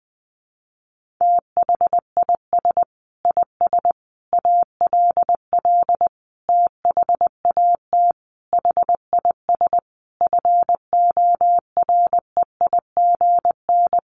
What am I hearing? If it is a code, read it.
Morse code, heard as a keyed tone, about 20 wpm, 700 Hz. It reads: THIS IS ALL THAT HIS FOREIGN
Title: Sentences from Top 1400 Words 20wpm